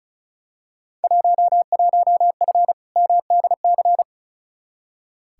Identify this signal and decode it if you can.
Morse code, heard as a keyed tone, about 36 wpm, 700 Hz. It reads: 11F MDC
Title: Field Day Exchanges Encore - 35wpm